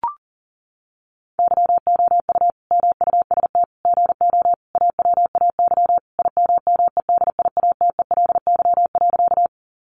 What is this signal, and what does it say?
Morse code, heard as a keyed tone, about 40 wpm, 700 Hz. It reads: YOU MUST GO AWAY IMMEDIATELY.
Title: Sentences from Top 1100 Words 40wpm